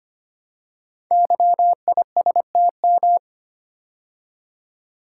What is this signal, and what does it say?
Morse code, heard as a keyed tone, about 25 wpm, 700 Hz. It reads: YISTM
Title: Code Groups - 5 Letters - 25wpm